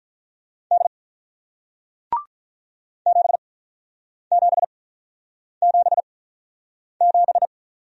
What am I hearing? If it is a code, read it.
Morse code, heard as a keyed tone, about 47 wpm, spaced as for 9 wpm, 700 Hz. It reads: D 7777